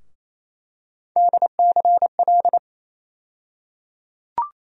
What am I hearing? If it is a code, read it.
Morse code, heard as a keyed tone, about 28 wpm, 700 Hz. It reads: DCL
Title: USA Call Sign Postfixes - 3 Letters 28wpm